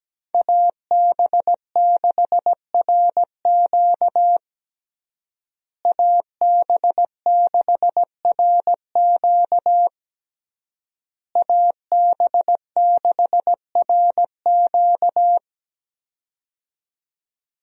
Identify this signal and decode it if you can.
Morse code, heard as a keyed tone, about 17 wpm, 700 Hz. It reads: AB6RQ AB6RQ AB6RQ
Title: USA Call Signs - 5 Characters - Repeated 3x - 17wpm